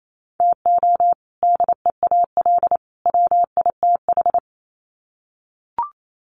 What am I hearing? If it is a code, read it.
Morse code, heard as a keyed tone, about 28 wpm, 700 Hz. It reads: TO DEAL WITH
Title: Sets of 3 Words 28wpm